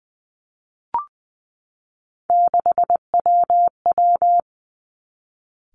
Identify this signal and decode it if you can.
Morse code, heard as a keyed tone, about 20 wpm, 700 Hz. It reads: 6WW
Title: Three Challenging Letter-Number Combinations 20wpm